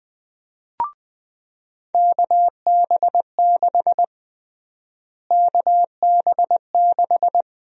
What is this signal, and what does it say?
Morse code, heard as a keyed tone, about 21 wpm, 700 Hz. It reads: KB6 KB6
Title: USA Call Sign Prefixes - Repeated 2x - 20wpm